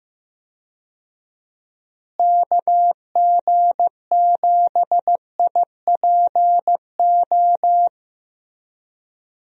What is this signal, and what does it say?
Morse code, heard as a keyed tone, about 15 wpm, 700 Hz. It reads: KG7IPO
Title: USA Call Signs - 6 Characters - Repeated 2x - 15wpm